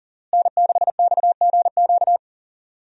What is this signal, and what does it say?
Morse code, heard as a keyed tone, about 37 wpm, 700 Hz. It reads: N6XGQ
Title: USA Call Signs - 5 Characters - Repeated 3x - 40wpm